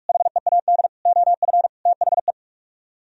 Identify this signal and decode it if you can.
Morse code, heard as a keyed tone, about 45 wpm, 700 Hz. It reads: HEAD OF THE